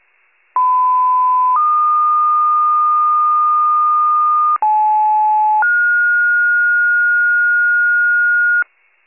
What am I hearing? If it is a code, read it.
Two-tone page, heard as a one-second tone, then a three-second tone, about 1010 then 1220 Hz, about 860 then 1430 Hz, with noise. It is heard from 0.6 to 8.6 s.